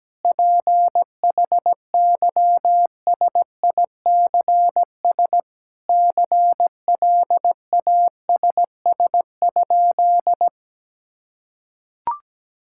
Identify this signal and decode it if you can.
Morse code, heard as a keyed tone, about 17 wpm, 700 Hz. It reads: PHYSICS CLASS?
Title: Sentences from Top 1000 Words - Review Part 5 - 17wpm